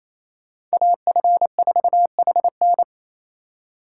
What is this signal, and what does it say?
Morse code, heard as a keyed tone, about 35 wpm, 700 Hz. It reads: AF4HN